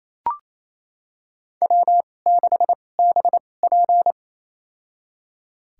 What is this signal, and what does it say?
Morse code, heard as a keyed tone, about 27 wpm, spaced as for 20 wpm, 690 Hz. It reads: W6BP
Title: USA Call Signs - 4 Characters - 2x Character Spacing - 28wpm